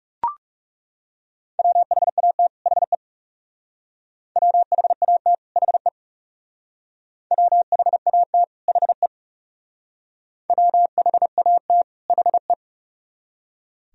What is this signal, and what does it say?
Morse code, heard as a keyed tone, about 38 wpm, 700 Hz. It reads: WHAT HE WHAT HE WHAT HE WHAT HE